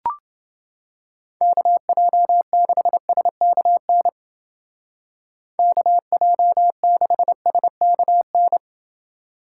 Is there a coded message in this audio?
Yes: Morse code from 1.4 to 8.6 s, heard as a keyed tone, about 29 wpm, 700 Hz. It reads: KJ6SKN KJ6SKN